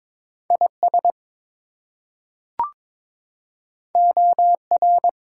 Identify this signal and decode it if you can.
Morse code, heard as a keyed tone, about 22 wpm, 700 Hz. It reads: IS OR
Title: Two Letter Words - 22wpm